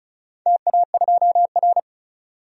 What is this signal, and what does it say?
Morse code, heard as a keyed tone, about 36 wpm, 700 Hz. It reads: TA2R
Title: DXCC-100 Call Signs - 4 Characters - Repeated 3x - 35wpm